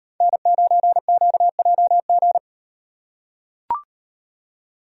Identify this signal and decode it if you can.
Morse code, heard as a keyed tone, about 38 wpm, 700 Hz. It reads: N9QJG